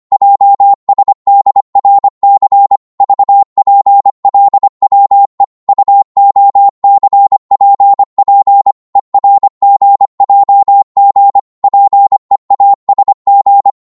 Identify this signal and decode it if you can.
Morse code, heard as a keyed tone, about 25 wpm, 800 Hz. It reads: JSDRC VPLWE UOCPP ERGJG PEASG
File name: Morse / ltr-25wpm